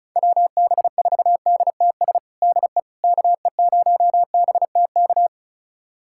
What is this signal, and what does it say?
Morse code, heard as a keyed tone, about 34 wpm, 690 Hz. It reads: WB4DTS DE KE0BTK
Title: USA Callsign DE Callsign - 6 Characters - 35wpm